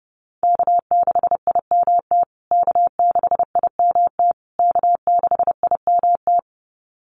Morse code, heard as a keyed tone, about 30 wpm, 700 Hz. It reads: K6IMT K6IMT K6IMT
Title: USA Call Signs Repeated 3x 30wpm